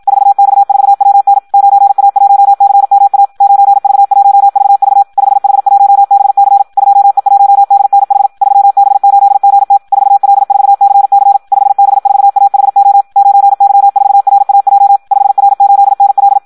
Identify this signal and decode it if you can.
Morse code, heard as a keyed tone, about 56 wpm, 800 Hz. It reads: 4X4WN 9A1CMA 9V1VV 5H1BC 2E1DNH 3B8GT 4L4CC 5B4AHO 9Y4RAJ 5R8NL
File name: Audio practica CW / qrq-tx-54 2